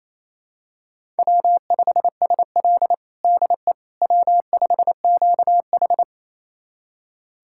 Morse code, heard as a keyed tone, about 27 wpm, 700 Hz. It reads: W5SL DE W5QH
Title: USA Callsign DE Callsign - 4 Characters - 28wpm